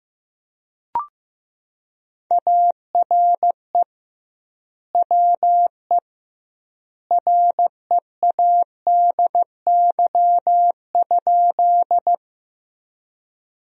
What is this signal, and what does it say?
Morse code, heard as a keyed tone, about 15 wpm, 700 Hz. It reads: ARE WE READY?
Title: 2x Word-Spacing - Sentences from Top 400 Words - 15wpm